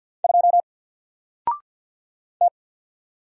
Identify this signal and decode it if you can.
Morse code, heard as a keyed tone, about 48 wpm, 700 Hz. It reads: 2 T
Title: Single Letter-Number Alt 50wpm